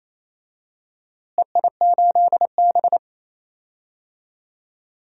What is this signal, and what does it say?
Morse code, heard as a keyed tone, about 28 wpm, 700 Hz. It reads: EI8B
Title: DXCC-100 Call Signs - 4 Characters - Repeated 3x - 28wpm